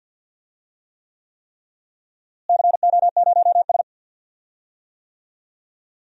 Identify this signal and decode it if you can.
Morse code, heard as a keyed tone, about 48 wpm, 700 Hz. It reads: CO0S
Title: DXCC-100 Call Signs - 4 Characters - Repeated 3x - 50wpm